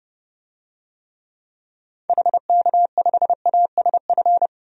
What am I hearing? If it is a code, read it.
Morse code, heard as a keyed tone, about 31 wpm, 700 Hz. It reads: HK5ASF
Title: DXCC-100 Call Signs - 6 Characters - 30wpm